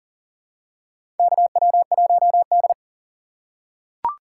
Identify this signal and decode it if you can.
Morse code, heard as a keyed tone, about 40 wpm, 700 Hz. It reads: KW1D
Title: USA Call Signs - 4x Character Spacing - 40wpm